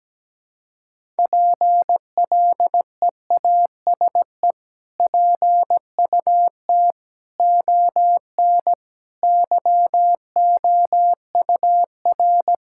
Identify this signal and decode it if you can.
Morse code, heard as a keyed tone, about 17 wpm, 700 Hz. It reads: PLEASE PUT ON YOUR
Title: Sentences from Top 1600 Words 17wpm